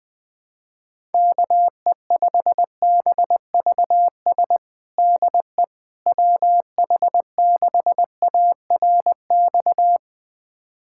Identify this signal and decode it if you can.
Morse code, heard as a keyed tone, about 20 wpm, 700 Hz. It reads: KE5BVS DE WH6ARX
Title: USA Callsign DE Callsign - 6 Characters - 20wpm